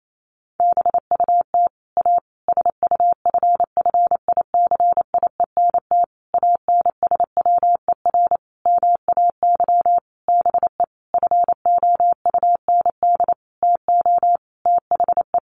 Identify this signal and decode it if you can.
Morse code, heard as a keyed tone, about 28 wpm, 700 Hz. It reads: BUT A SUFFICIENT ANSWER MAY BE FOUND TO THE
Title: Sentences from Top 1400 Words 28wpm